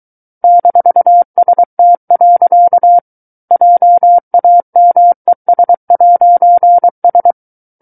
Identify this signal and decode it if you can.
Morse code, heard as a keyed tone, about 23 wpm, 700 Hz. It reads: -ST. JAMES'S
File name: Book One / 14-Chapter 14 - 23 WPM